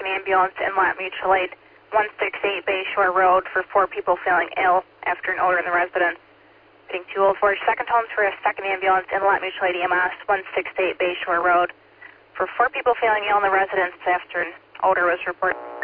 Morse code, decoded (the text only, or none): none